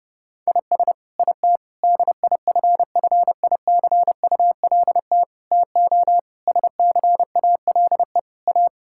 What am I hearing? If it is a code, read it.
Morse code, heard as a keyed tone, about 32 wpm, 700 Hz. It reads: IS IT DIFFICULT TO SCALE A
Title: Sentences from Top 1000 Words - Review Part 5 - 30wpm